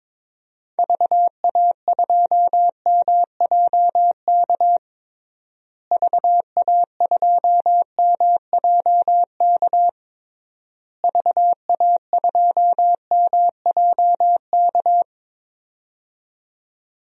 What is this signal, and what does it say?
Morse code, heard as a keyed tone, about 22 wpm, 700 Hz. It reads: VA2MJK VA2MJK VA2MJK
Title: DXCC-100 Call Signs - 6 Characters - Repeated 3x - 22wpm